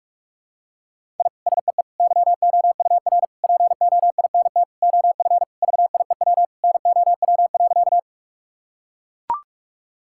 Morse code, heard as a keyed tone, about 42 wpm, 700 Hz. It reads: I SEE YOUR POINT OF VIEW NOW.
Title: Sentences from Top 1000 Words - Review Part 6 - 45wpm